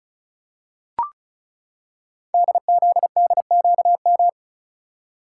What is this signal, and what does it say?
Morse code, heard as a keyed tone, about 36 wpm, 700 Hz. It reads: DZDQM
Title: Code Groups - 5 Letters - 35wpm